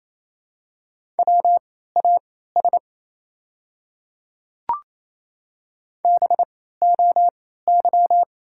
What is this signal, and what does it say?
Morse code, heard as a keyed tone, about 28 wpm, spaced as for 16 wpm, 700 Hz. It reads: WAS BOY
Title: 3x Character Spacing - Top 300 Words - 28wpm